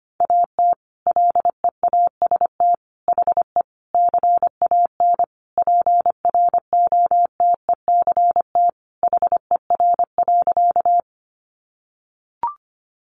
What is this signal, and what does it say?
Morse code, heard as a keyed tone, about 25 wpm, 700 Hz. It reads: AT LEAST HE CAN PROTECT HER.
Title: Sentences from Top 900 Words 25wpm